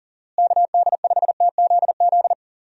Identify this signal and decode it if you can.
Morse code, heard as a keyed tone, about 40 wpm, 700 Hz. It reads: KD5TZZ
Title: USA Call Signs - 6 Characters - Repeated 2x - 40wpm